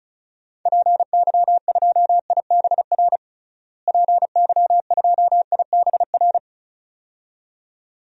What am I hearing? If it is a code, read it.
Morse code, heard as a keyed tone, about 34 wpm, 700 Hz. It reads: PY2IBR PY2IBR
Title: DXCC-100 Call Signs - 6 Characters - Repeated 2x - 35wpm